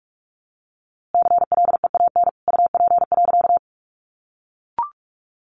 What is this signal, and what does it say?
Morse code, heard as a keyed tone, about 46 wpm, 700 Hz. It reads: CLEAN UP.
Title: A Sound Of Thunder 45wpm